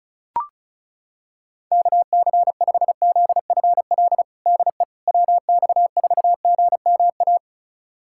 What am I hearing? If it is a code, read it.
Morse code, heard as a keyed tone, about 34 wpm, 690 Hz. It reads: KC5ZFL DE WX4GMA
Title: USA Callsign DE Callsign - 6 Characters - 35wpm